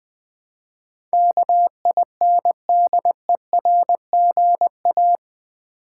Morse code, heard as a keyed tone, about 20 wpm, 700 Hz. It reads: KINDERGA
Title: Build A Word - 4 Letters At A Time - 12-Letter Words - 20wpm